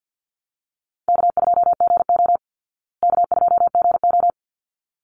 Morse code, heard as a keyed tone, about 50 wpm, 700 Hz. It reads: K2GO K2GO
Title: Call Signs Repeated 3x - 50wpm